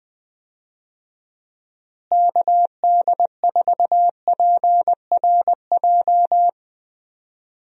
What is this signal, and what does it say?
Morse code, heard as a keyed tone, about 20 wpm, 700 Hz. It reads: KD4PRJ
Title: USA Call Signs - 6 Characters - Repeated 2x - 20wpm